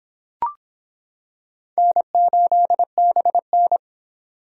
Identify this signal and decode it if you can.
Morse code, heard as a keyed tone, about 25 wpm, 700 Hz. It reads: N8BN